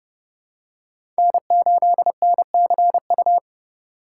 Morse code, heard as a keyed tone, about 31 wpm, 700 Hz. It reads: N8NCU
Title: USA Call Signs - 5 Characters - 3x Character Spacing - 30wpm